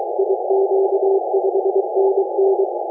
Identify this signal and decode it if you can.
Morse code, heard as a keyed tone, about 23 wpm, 390 Hz, with noise. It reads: IQ5NN